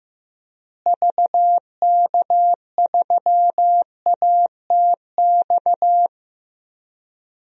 Morse code, heard as a keyed tone, about 15 wpm, 700 Hz. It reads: VK3ATX
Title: DXCC-100 Call Signs - 6 Characters - Repeated 3x - 15wpm